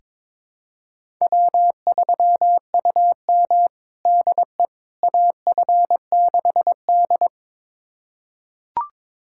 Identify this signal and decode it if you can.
Morse code, heard as a keyed tone, about 22 wpm, 700 Hz. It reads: W3UM DE AF6D
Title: USA Callsign DE Callsign - 4 Characters - 22wpm